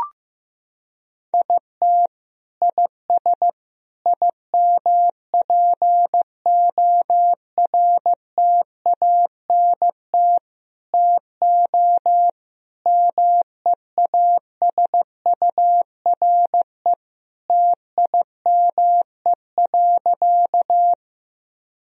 Morse code, heard as a keyed tone, about 15 wpm, 700 Hz. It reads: IT IS IMPORTANT TO MEASURE TIME.
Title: Sentences from Top 1000 Words - Review Part 2 - 15wpm